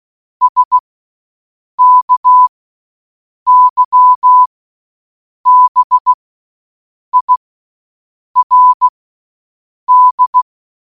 Morse code, heard as a keyed tone, about 16 wpm, spaced as for 7 wpm, 1000 Hz. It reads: SKYBIRD